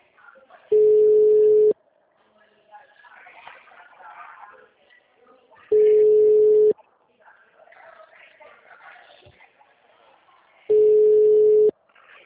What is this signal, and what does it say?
Ringback tone (Europe, 425 Hz), with speech.
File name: out- -2002- - - .49825